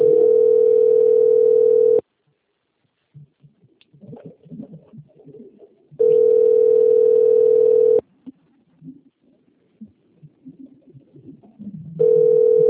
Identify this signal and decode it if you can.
Ringback tone (North America), 440 + 480 Hz.